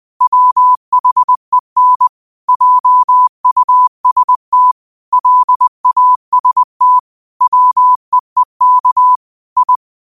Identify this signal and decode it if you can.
Morse code, heard as a keyed tone, about 20 wpm, 1000 Hz. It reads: WHEN JUST LAST WEEK I
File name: qotdfiles / 20wpm